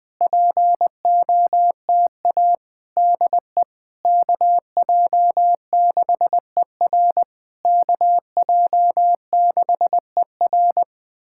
Morse code, heard as a keyed tone, about 20 wpm, 700 Hz. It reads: POTA DE KJ6ER KJ6ER
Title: POTA QSO Practice - 20wpm